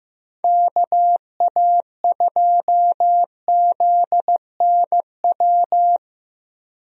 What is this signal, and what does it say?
Morse code, heard as a keyed tone, about 15 wpm, 700 Hz. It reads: KA2ZNW
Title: USA Call Signs - 6 Characters - 15wpm